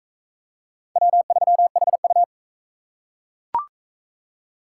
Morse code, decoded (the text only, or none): W3HU